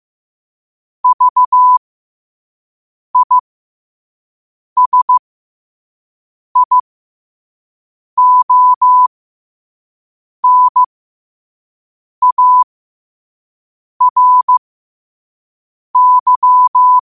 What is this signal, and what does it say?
Morse code, heard as a keyed tone, about 15 wpm, spaced as for 5 wpm, 1000 Hz. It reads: VISIONARY
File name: qotdfiles / 5wpm